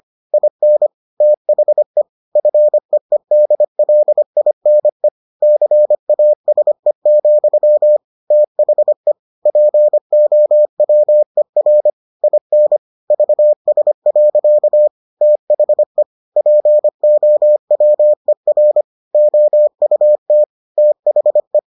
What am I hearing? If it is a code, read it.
Morse code, heard as a keyed tone, about 25 wpm, 600 Hz. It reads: IN THE FEEDLINE CASE, THE POWER IN VS. THE POWER OUT THE